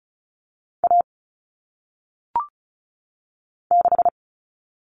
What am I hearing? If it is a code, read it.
Morse code, heard as a keyed tone, about 36 wpm, 700 Hz. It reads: A 6
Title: Common Cut Numbers - 1-Digit 35wpm